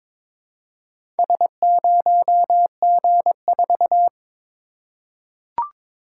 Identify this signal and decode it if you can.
Morse code, heard as a keyed tone, about 22 wpm, 700 Hz. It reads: S0G4
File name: practice / 22wpm / Four Letter-Number Combinations - 22wpm